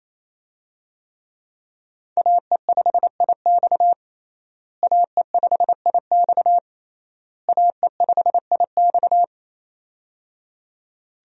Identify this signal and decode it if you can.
Morse code, heard as a keyed tone, about 28 wpm, 700 Hz. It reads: AE5IX AE5IX AE5IX
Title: USA Call Signs - 5 Characters - Repeated 3x - 28wpm